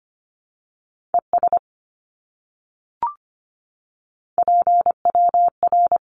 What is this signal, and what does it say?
Morse code, heard as a keyed tone, about 25 wpm, 700 Hz. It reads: ES PWR
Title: Abbreviations Repeated 3x - 25wpm